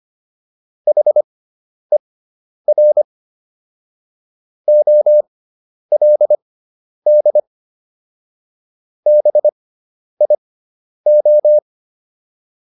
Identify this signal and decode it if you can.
Morse code, heard as a keyed tone, about 25 wpm, spaced as for 10 wpm, 600 Hz. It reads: HER OLD BIO